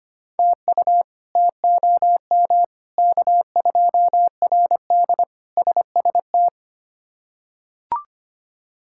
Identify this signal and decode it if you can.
Morse code, heard as a keyed tone, about 25 wpm, 700 Hz. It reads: TU TOM K2RD SST
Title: K1USN SST QSO Practice - 25wpm